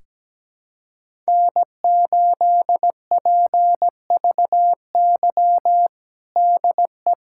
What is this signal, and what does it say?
Morse code, heard as a keyed tone, about 17 wpm, 700 Hz. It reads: N8PVY DE
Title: USA Callsign DE Callsign - 5 Characters - 17wpm